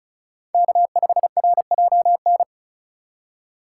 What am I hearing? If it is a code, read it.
Morse code, heard as a keyed tone, about 36 wpm, 700 Hz. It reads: K5RJN
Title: USA Call Signs - 4x Character Spacing - 35wpm